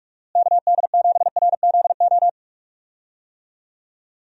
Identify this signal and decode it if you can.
Morse code, heard as a keyed tone, about 42 wpm, 700 Hz. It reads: KD7RZO
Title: USA Call Signs - 6 Characters - Repeated 3x - 45wpm